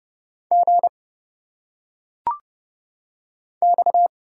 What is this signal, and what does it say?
Morse code, heard as a keyed tone, about 29 wpm, 700 Hz. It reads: G X
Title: Single Letters - Phonetic Pronunciation 30wpm